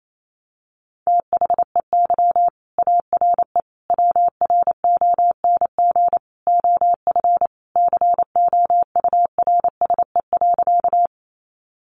Morse code, heard as a keyed tone, about 28 wpm, 700 Hz. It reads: THEY ARE WRONG OF COURSE.
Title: Sentences from Top 900 Words 28wpm